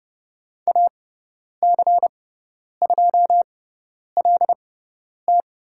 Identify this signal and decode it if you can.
Morse code, heard as a keyed tone, about 30 wpm, spaced as for 10 wpm, 700 Hz. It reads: AC2LT
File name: Lesson 16g - Letter c - Call Signs 30wpm